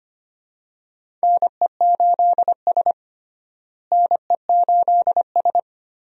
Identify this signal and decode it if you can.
Morse code, heard as a keyed tone, about 25 wpm, 700 Hz. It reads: NE8S NE8S
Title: USA Call Signs - 4 Characters - Repeated 3x - 25wpm